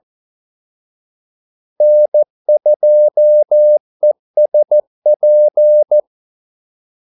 Morse code, heard as a keyed tone, about 14 wpm, 600 Hz. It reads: N2ESP
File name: Callsigns-14wpm